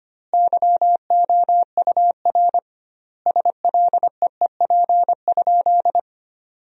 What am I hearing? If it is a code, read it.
Morse code, heard as a keyed tone, about 25 wpm, 700 Hz. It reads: YOUR SLEEP?
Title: 2x Word-Spacing - Sentences from Top 500 Words - 25wpm